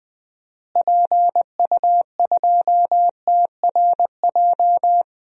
Morse code, heard as a keyed tone, about 20 wpm, 700 Hz. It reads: PU2TRJ
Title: DXCC-100 Call Signs - 6 Characters - Repeated 3x - 20wpm